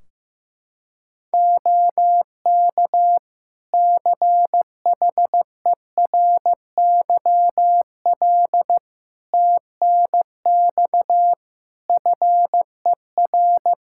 Morse code, heard as a keyed tone, about 15 wpm, 700 Hz. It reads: OK CHERYL TNX FER
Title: Practice QSO - Exchange 4 - Ending - 15wpm